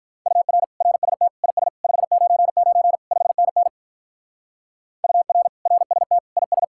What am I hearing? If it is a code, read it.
Morse code, heard as a keyed tone, about 43 wpm, 700 Hz. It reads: UR RST IS 599 5NN UR RST IS
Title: Speed-Racing Common QSO Phrases - 35wpm